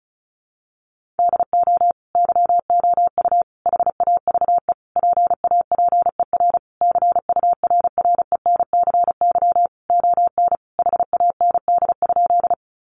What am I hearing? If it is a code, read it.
Morse code, heard as a keyed tone, about 34 wpm, 700 Hz. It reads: DO YOU HAVE PAPER CURRENCY ON HAND?
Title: Sentences from Top 1500 Words 35wpm